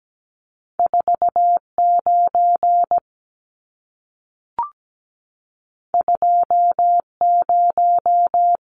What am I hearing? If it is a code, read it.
Morse code, heard as a keyed tone, about 17 wpm, 700 Hz. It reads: 49 20
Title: Numbers - 2-Digit 17wpm